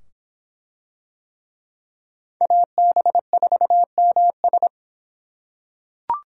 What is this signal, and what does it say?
Morse code, heard as a keyed tone, about 26 wpm, 700 Hz. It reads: AB4MS